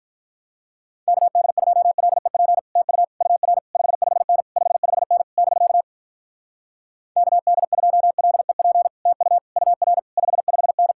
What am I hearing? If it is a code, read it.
Morse code, heard as a keyed tone, about 46 wpm, 700 Hz. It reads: KD2LEP TU UR 55N 55N <BK> KD2LEP TU UR 55N